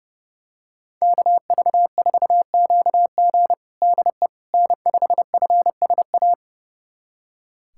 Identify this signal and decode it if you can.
Morse code, heard as a keyed tone, about 32 wpm, 700 Hz. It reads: KV4QG DE N5FSA
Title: USA Callsign DE Callsign - 5 Characters - 30wpm